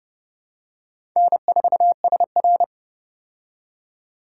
Morse code, heard as a keyed tone, about 31 wpm, 700 Hz. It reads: N4SR
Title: K1USN SST QSO Practice - 30wpm